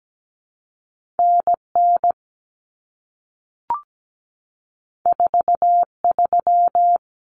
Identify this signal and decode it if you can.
Morse code, heard as a keyed tone, about 17 wpm, 700 Hz. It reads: NN 43
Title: Common Cut Numbers - 2-Digit 17wpm